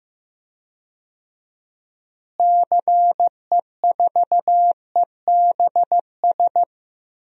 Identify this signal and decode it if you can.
Morse code, heard as a keyed tone, about 15 wpm, 700 Hz. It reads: CE4EBS
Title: DXCC-100 Call Signs - 6 Characters - Repeated 2x - 15wpm